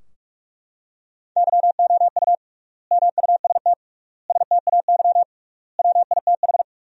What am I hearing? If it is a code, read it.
Morse code, heard as a keyed tone, about 42 wpm, 700 Hz. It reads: YOU MUST STAY WITH
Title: 3x Word-Spacing - Sentences from Top 500 Words - 45wpm